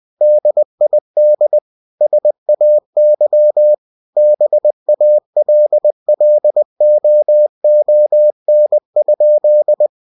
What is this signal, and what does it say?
Morse code, heard as a keyed tone, about 20 wpm, 600 Hz. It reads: DID SAY BALLOON?